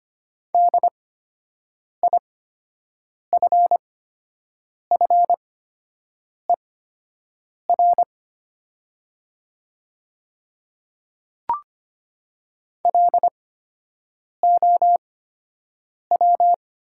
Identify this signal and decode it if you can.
Morse code, heard as a keyed tone, about 25 wpm, spaced as for 7 wpm, 700 Hz. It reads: DIFFER LOW
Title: 8x Character Spacing - Top 200 Words - 25wpm